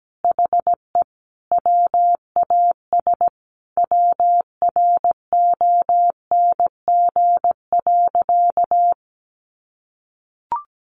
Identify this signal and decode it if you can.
Morse code, heard as a keyed tone, about 17 wpm, 700 Hz. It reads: HE WAS WRONG.
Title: Sentences from Top 1100 Words 17wpm